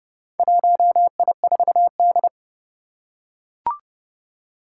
Morse code, heard as a keyed tone, about 29 wpm, 700 Hz. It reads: 1I4D
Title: Four Challenging Letter-Number Combinations 30wpm